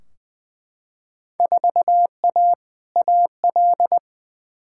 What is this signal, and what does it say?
Morse code, heard as a keyed tone, about 20 wpm, 700 Hz. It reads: 4A AL
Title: Field Day Exchanges - 20wpm